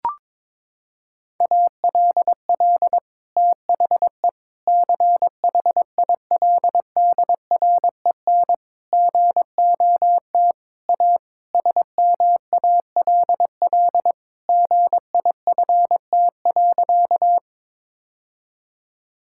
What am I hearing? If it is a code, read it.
Morse code, heard as a keyed tone, about 22 wpm, 700 Hz. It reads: ALL THE CHILDREN GOT A SMALL GIFT.
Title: Sentences from Top 1000 Words - Review Part 6 - 22wpm